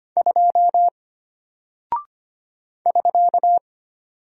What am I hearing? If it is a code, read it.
Morse code, heard as a keyed tone, about 25 wpm, 700 Hz. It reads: 2 <SK>
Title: Single Character (Letter:Number:Punctuation:Prosign) Alt 25wpm